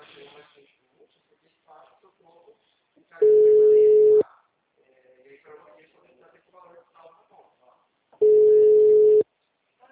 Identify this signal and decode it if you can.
Ringback tone (Europe, 425 Hz), with speech.